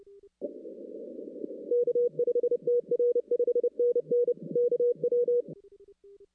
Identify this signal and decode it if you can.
Morse code, heard as a keyed tone, about 27 wpm, 480 Hz. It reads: K5TR 5NN KW
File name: 2016 arrl dx cw / 2016-ARRL-DX-CW-K5TR- - 0-25